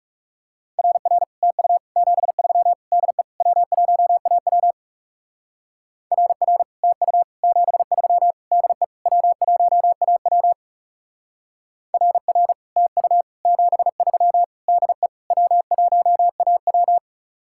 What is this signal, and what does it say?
Morse code, heard as a keyed tone, about 37 wpm, 700 Hz. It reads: RR TU 73 DE W1AW RR TU 73 DE W1AW RR TU 73 DE W1AW
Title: Speed-Racing Common QSO Phrases Encore - 30wpm